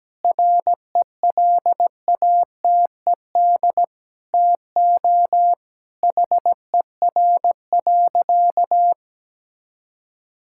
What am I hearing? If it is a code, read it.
Morse code, heard as a keyed tone, about 17 wpm, 700 Hz. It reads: RELATED TO HER.
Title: Sentences from Top 1000 Words - Review Part 5 - 17wpm